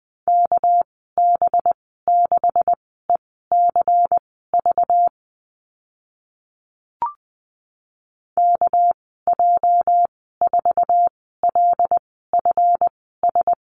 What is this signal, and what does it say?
Morse code, heard as a keyed tone, about 20 wpm, spaced as for 15 wpm, 700 Hz. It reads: KB6ECV KJ4LFS
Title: 231 Lesson 37g - Letter x - Call Signs 20wpm